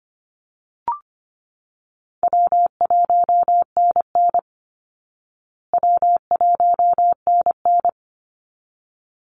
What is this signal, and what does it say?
Morse code, heard as a keyed tone, about 25 wpm, 700 Hz. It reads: W1NN W1NN
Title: Call Signs Repeated 3x - 25wpm